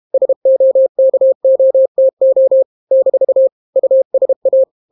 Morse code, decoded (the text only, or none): SOKOTO - USA